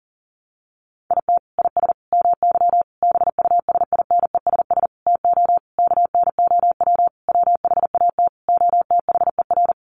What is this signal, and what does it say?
Morse code, heard as a keyed tone, about 40 wpm, 700 Hz. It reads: IT IS MY BUSINESS TO KNOW WHAT OTHER
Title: Sentences from Top 600 Words 40wpm